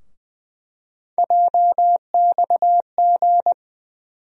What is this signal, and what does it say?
Morse code, heard as a keyed tone, about 21 wpm, 700 Hz. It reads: JXG
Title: USA Call Sign Postfixes - 3 Letters 20wpm